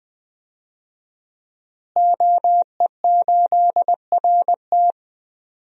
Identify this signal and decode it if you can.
Morse code, heard as a keyed tone, about 20 wpm, 700 Hz. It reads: OE8RT
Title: DXCC-100 Call Signs - 5 Characters - Repeated 3x - 20wpm